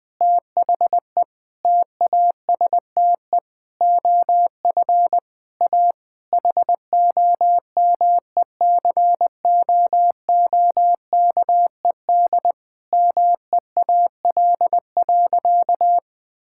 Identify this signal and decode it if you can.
Morse code, heard as a keyed tone, about 20 wpm, 700 Hz. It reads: THE TASTE OF A HOMECOOKED MEAL.
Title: Difficult Sentences with Hints - 20wpm